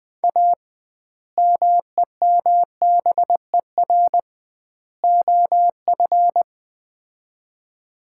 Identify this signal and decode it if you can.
Morse code, heard as a keyed tone, about 20 wpm, 700 Hz. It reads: A MEMBER OF
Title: 2x Word-Spacing - Sets of 3 Words - 20wpm